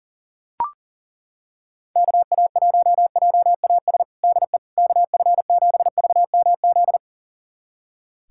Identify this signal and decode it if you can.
Morse code, heard as a keyed tone, about 40 wpm, 700 Hz. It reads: KA1JAS DE KF7VMZ
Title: USA Callsign DE Callsign - 6 Characters - 40wpm